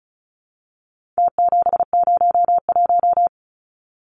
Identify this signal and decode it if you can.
Morse code, heard as a keyed tone, about 34 wpm, 700 Hz. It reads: T701
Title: Numbers - Contest Style - 4-Digit 35wpm